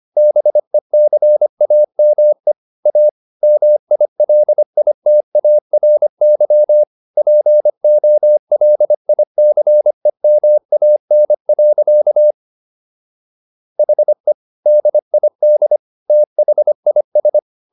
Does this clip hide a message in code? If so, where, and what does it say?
Yes: Morse code from 0.2 to 17.4 s, heard as a keyed tone, about 25 wpm, 600 Hz. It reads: BECAME A MILITARY POLICEMAN. HE DID THIS